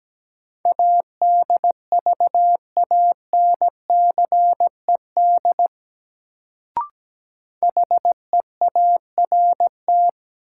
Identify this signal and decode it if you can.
Morse code, heard as a keyed tone, about 17 wpm, 700 Hz. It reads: ADVANCED HEART
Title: Sentences from Top 1200 Words - New Word Drill Alt - 17wpm